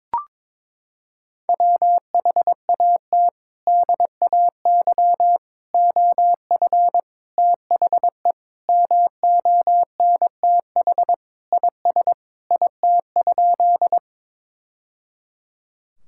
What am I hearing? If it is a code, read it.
Morse code, heard as a keyed tone, about 22 wpm, 700 Hz. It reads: WHAT DAY OF THE MONTH IS IT?
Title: Sentences from Top 1000 Words - Review Part 5 - 22wpm